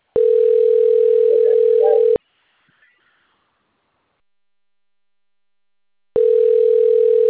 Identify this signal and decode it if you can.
Ringback tone (North America), 440 + 480 Hz.